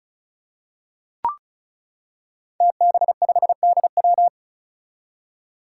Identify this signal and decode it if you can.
Morse code, heard as a keyed tone, about 36 wpm, 700 Hz. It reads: TB5DW
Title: Five Challenging Letter-Number Combinations 35wpm